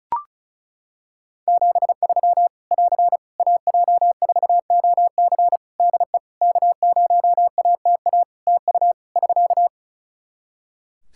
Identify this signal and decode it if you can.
Morse code, heard as a keyed tone, about 36 wpm, 700 Hz. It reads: 73 <AR> AJ4OC DE K0ATA TU <SK>
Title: Practice QSO - Exchange 4 - Ending - 35wpm